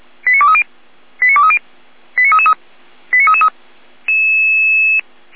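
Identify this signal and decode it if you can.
Five-tone selective call (ZVEI), heard as quick runs of five tones, tones of about 80 ms. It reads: 89230 89230 89333 89333
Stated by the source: ZVEI five-tone selective call 89294 received by SDR.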